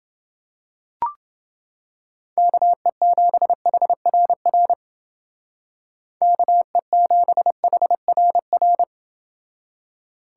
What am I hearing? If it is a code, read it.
Morse code, heard as a keyed tone, about 29 wpm, 700 Hz. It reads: KE7HRR KE7HRR